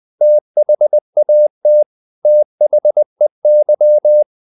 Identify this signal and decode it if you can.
Morse code, heard as a keyed tone, about 20 wpm, 600 Hz. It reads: THAT THEY
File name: the old man and the sea / the old man and the sea part 1 20wpm